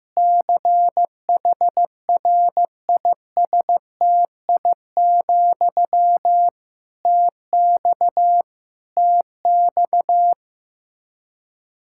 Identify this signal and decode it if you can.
Morse code, heard as a keyed tone, about 15 wpm, 700 Hz. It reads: CHRISTI, TX TX
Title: Practice QSO - Exchange 1 - Beginning - 15wpm